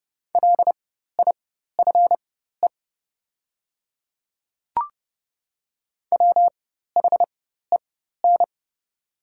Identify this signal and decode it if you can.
Morse code, heard as a keyed tone, about 30 wpm, spaced as for 14 wpm, 700 Hz. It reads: LIFE WHEN
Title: 4x Character Spacing - Top 500 Words - 30wpm